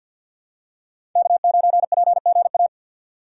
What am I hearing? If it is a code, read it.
Morse code, heard as a keyed tone, about 50 wpm, 700 Hz. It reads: K9PGA
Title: USA Call Signs - 3x Character Spacing - 50wpm